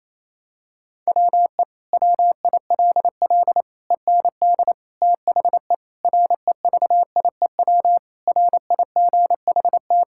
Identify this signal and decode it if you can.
Morse code, heard as a keyed tone, about 27 wpm, 700 Hz. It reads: WE WILL END THE REVIEW RIGHT
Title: Sentences from Top 1000 Words - Review Part 6 - 28wpm